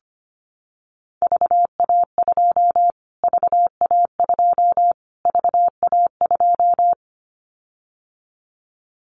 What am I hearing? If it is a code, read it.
Morse code, heard as a keyed tone, about 25 wpm, 700 Hz. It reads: VA2 VA2 VA2
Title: Callsign Prefixes for Top 25 DXCC - Repeated 3x - 25wpm